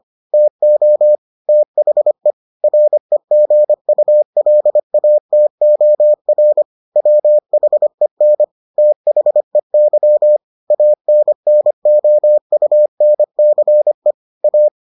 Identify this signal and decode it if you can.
Morse code, heard as a keyed tone, about 25 wpm, 600 Hz. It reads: TO THE REGULATOR WHEN THEY ANNOUNCE A